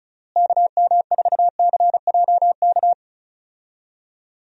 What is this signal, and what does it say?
Morse code, heard as a keyed tone, about 34 wpm, 700 Hz. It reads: KM4CJK
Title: USA Call Signs - 6 Characters - Repeated 3x - 35wpm